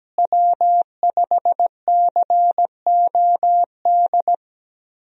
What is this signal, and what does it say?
Morse code, heard as a keyed tone, about 17 wpm, 700 Hz. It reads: W5COD
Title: USA Call Signs - 5 Characters - Repeated 3x - 17wpm